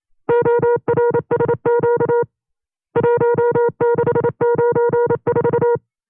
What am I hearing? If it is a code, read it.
Morse code, heard as a keyed tone, about 28 wpm, 480 Hz. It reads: ORSQ 1694
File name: 25WPM